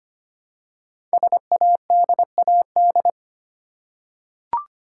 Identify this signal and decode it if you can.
Morse code, heard as a keyed tone, about 25 wpm, 700 Hz. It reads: SADAD
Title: Code Groups - 5 Letters - 25wpm